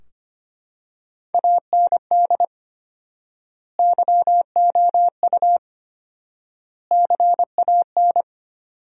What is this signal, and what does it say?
Morse code, heard as a keyed tone, about 25 wpm, 700 Hz. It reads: AND YOU CAN
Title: 4x Word-Spacing - Sentences from Top 400 Words - 25wpm